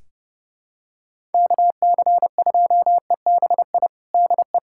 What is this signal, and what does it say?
Morse code, heard as a keyed tone, about 32 wpm, 700 Hz. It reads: KC2EBI DE
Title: USA Callsign DE Callsign 30wpm